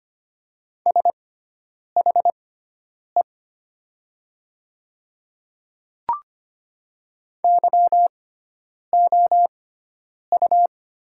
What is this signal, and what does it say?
Morse code, heard as a keyed tone, about 25 wpm, spaced as for 9 wpm, 700 Hz. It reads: SHE YOU
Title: 6x Character Spacing - Top 200 Words - 25wpm